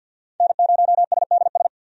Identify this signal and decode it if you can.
Morse code, heard as a keyed tone, about 53 wpm, 700 Hz. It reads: N0SDS
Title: USA Call Signs - 5 Characters - 50wpm